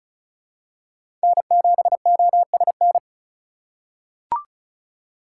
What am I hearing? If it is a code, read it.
Morse code, heard as a keyed tone, about 34 wpm, 700 Hz. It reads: N7OSN
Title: USA Call Signs - 5 Characters - 4x Character Spacing - 35wpm